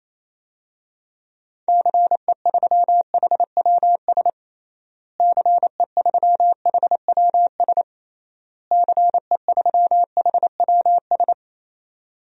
Morse code, heard as a keyed tone, about 28 wpm, 700 Hz. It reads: CE3HWS CE3HWS CE3HWS
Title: DXCC-100 Call Signs - 6 Characters - Repeated 3x - 28wpm